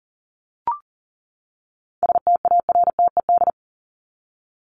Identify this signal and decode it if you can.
Morse code, heard as a keyed tone, about 40 wpm, 700 Hz. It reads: STARTED
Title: ED Suffix - Top 100 Words with Base Words 40wpm